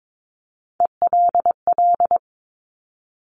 Morse code, heard as a keyed tone, about 22 wpm, 700 Hz. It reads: ELL
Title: Top Three Letter Combinations in Words 22wpm